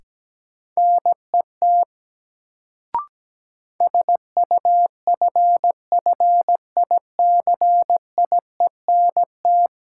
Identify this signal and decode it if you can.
Morse code, heard as a keyed tone, about 17 wpm, 700 Hz. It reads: NET SUFFICIENT
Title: Sentences from Top 1400 Words - New Word Drill - 17wpm